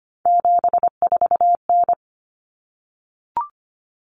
Morse code, heard as a keyed tone, about 25 wpm, 700 Hz. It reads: 74N
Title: Common Cut Numbers - 3-Digit 25wpm